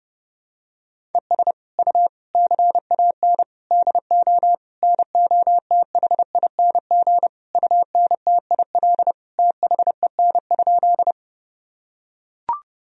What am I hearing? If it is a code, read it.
Morse code, heard as a keyed tone, about 28 wpm, 700 Hz. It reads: ES U CAN DO NOTHING UNTIL THEN?
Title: Sentences from Top 500 Words and Abbreviations 30wpm